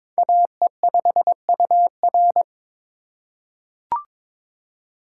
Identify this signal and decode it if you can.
Morse code, heard as a keyed tone, about 22 wpm, 700 Hz. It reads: AE5UR
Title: USA Call Signs - 5 Characters - 4x Character Spacing - 22wpm